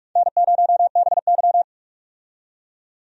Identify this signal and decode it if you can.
Morse code, heard as a keyed tone, about 46 wpm, 700 Hz. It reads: N0BY
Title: USA Call Signs - 4 Characters - 45wpm